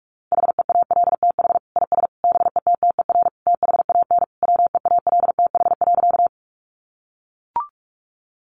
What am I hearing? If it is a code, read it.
Morse code, heard as a keyed tone, about 44 wpm, 700 Hz. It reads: HEALTH IS BETTER THAN WEALTH.
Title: Sentences from Top 1200 Words 45wpm